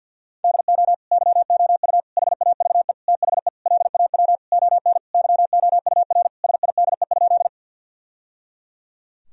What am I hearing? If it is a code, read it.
Morse code, heard as a keyed tone, about 50 wpm, 700 Hz. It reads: DO YOU HAVE THE LAW ON YOUR SIDE?
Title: Sentences from Top 1000 Words - Review Part 3 - 50wpm